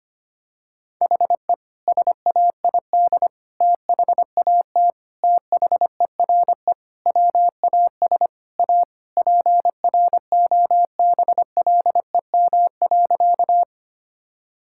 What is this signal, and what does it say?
Morse code, heard as a keyed tone, about 25 wpm, 700 Hz. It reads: HE SAID THAT THERE WAS A PROBLEM.
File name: practice / 25wpm / Sentences with 3-Word Phrases - 25wpm